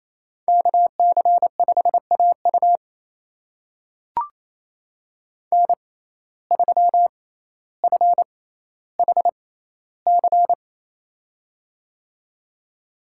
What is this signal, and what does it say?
Morse code, heard as a keyed tone, about 28 wpm, spaced as for 14 wpm, 700 Hz. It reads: KC5AU N3FHC
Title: USA Call Signs - 5 Characters - 6x Character Spacing - 28wpm